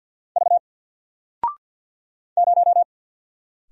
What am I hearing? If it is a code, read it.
Morse code, heard as a keyed tone, about 48 wpm, 700 Hz. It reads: V 0
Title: Single Character (Letter:Number:Punctuation:Prosign) Alt 50wpm